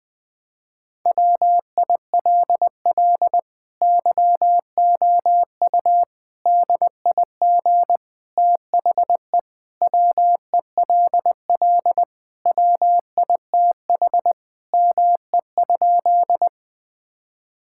Morse code, heard as a keyed tone, about 20 wpm, 700 Hz. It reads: WILL YOU DIG THE WELL WITH ME?
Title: Sentences from Top 1600 Words 20wpm